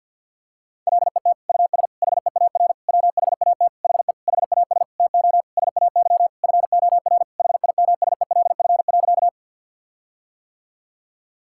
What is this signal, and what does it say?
Morse code, heard as a keyed tone, about 52 wpm, 700 Hz. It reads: LET US HEAR WHAT HE HAS TO SAY FOR HIMSELF.
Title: Sentences from Top 1600 Words 50wpm